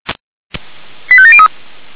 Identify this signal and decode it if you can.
Five-tone selective call (ZVEI), heard as quick runs of five tones, tones of about 80 ms. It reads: 85703